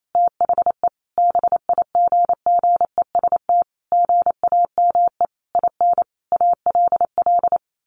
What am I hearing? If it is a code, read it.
Morse code, heard as a keyed tone, about 28 wpm, 700 Hz. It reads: THE BIGGEST GAME IN ALL
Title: A Sound Of Thunder 28wpm